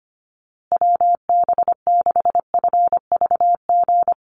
Morse code, heard as a keyed tone, about 25 wpm, 700 Hz. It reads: WB6FVG
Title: USA Call Signs Repeated 3x 25wpm